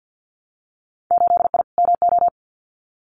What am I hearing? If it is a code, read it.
Morse code, heard as a keyed tone, about 50 wpm, 700 Hz. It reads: 8I MO
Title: Winter Field Day Exchanges Encore 50wpm